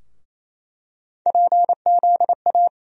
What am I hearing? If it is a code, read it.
Morse code, heard as a keyed tone, about 27 wpm, 700 Hz. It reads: PZA
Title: USA Call Sign Postfixes - 3 Letters 28wpm